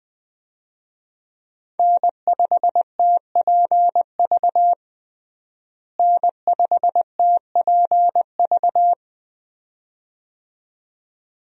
Morse code, heard as a keyed tone, about 20 wpm, 700 Hz. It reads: N5TPV N5TPV
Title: USA Call Signs - 5 Characters - Repeated 2x - 20wpm